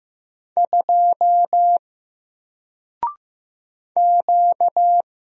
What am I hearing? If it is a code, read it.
Morse code, heard as a keyed tone, about 15 wpm, 700 Hz. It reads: 2 Q
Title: Single Letter-Number 15wpm